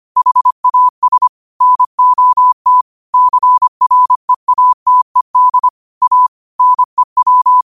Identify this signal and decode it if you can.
Morse code, heard as a keyed tone, about 25 wpm, 1000 Hz. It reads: HAS NOT CREATED A NEW
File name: qotdfiles / 25wpm